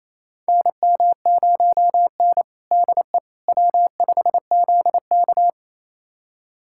Morse code, heard as a keyed tone, about 28 wpm, 700 Hz. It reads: NM0N DE W5ZK
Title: USA Callsign DE Callsign - 4 Characters - 28wpm